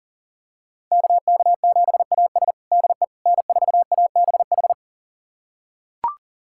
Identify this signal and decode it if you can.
Morse code, heard as a keyed tone, about 40 wpm, 700 Hz. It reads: KK7AS DE N4ABH
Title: USA Callsign DE Callsign - 5 Characters - 40wpm